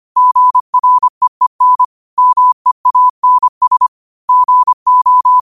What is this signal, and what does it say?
Morse code, heard as a keyed tone, about 25 wpm, 1000 Hz. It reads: GREEN MEANS GO